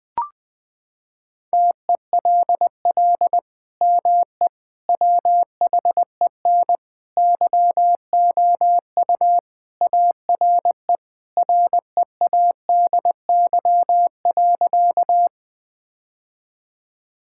Morse code, heard as a keyed tone, about 21 wpm, 700 Hz. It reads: TELL ME WHEN YOU ARE READY.
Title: Sentences with 2-Word Phrases - 20wpm